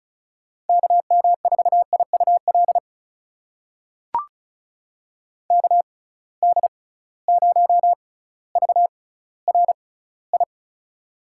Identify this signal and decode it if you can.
Morse code, heard as a keyed tone, about 36 wpm, spaced as for 18 wpm, 700 Hz. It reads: KM4IUL KD0VRI
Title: USA Call Signs - 6 Characters - 6x Character Spacing - 35wpm